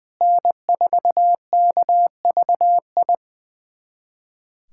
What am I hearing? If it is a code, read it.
Morse code, heard as a keyed tone, about 20 wpm, 700 Hz. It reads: N4KVI
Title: USA Callsign DE Callsign - 5 Characters - 20wpm